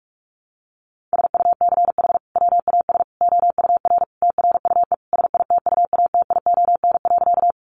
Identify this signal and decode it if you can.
Morse code, heard as a keyed tone, about 44 wpm, 700 Hz. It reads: SUCH WAS OUR TRUE SITUATION.
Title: Sentences from Top 1200 Words 45wpm